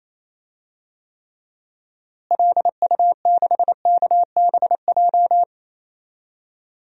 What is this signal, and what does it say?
Morse code, heard as a keyed tone, about 28 wpm, 700 Hz. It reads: LU6KBJ
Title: DXCC-100 Call Signs - 6 Characters - Repeated 3x - 28wpm